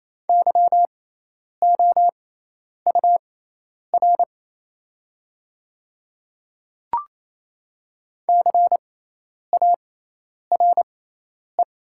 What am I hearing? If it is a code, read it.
Morse code, heard as a keyed tone, about 28 wpm, spaced as for 10 wpm, 700 Hz. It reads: YOUR CARE
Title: 6x Character Spacing - Top 300 Words - 28wpm